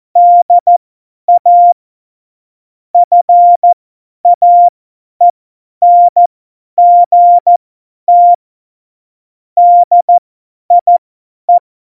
Morse code, heard as a keyed tone, about 14 wpm, spaced as for 10 wpm, 700 Hz. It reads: DA FAENGT DIE
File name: Morsekurs / Witze / Witz 21. August 2018, 50 F70